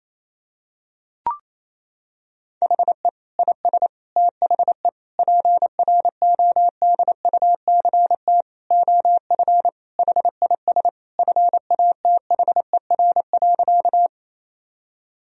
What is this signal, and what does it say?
Morse code, heard as a keyed tone, about 28 wpm, 700 Hz. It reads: HE IS THE PRODUCT OF HIS FATHER.
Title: Sentences from Top 1000 Words - Review Part 2 - 28wpm